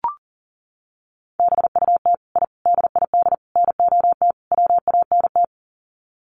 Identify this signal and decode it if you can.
Morse code, heard as a keyed tone, about 40 wpm, 700 Hz. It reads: BUT I DID NOT WANT
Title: Sets of 5 Words Encore 40wpm